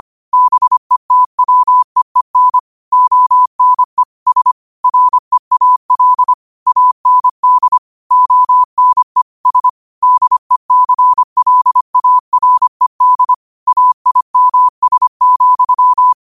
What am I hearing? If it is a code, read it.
Morse code, heard as a keyed tone, about 25 wpm, 1000 Hz. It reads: BETWEEN ONE S REAL AND ONE S DECLARED AIMS,
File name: qotdfiles / 25wpm